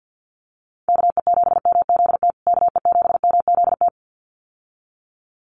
Morse code, heard as a keyed tone, about 48 wpm, 700 Hz. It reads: KE7MZT KE7MZT
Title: USA Call Signs Repeated 2x 50wpm